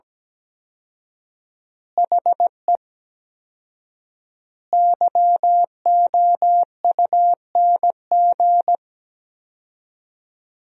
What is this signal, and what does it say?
Morse code, heard as a keyed tone, about 17 wpm, 700 Hz. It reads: HE YOUNG
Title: 4x Word-Spacing - Sentences from Top 400 Words - 17wpm